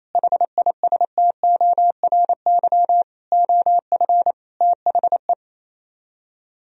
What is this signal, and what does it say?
Morse code, heard as a keyed tone, about 35 wpm, 700 Hz. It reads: HISTORY OF THE